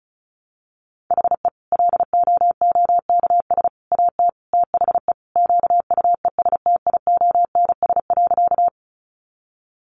Morse code, heard as a keyed tone, about 34 wpm, 700 Hz. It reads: HE LOOKS AT THE QUESTIONS.
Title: Additional Easy Phrases 35wpm